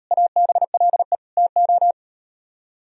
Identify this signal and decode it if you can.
Morse code, heard as a keyed tone, about 39 wpm, 700 Hz. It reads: ABLE TO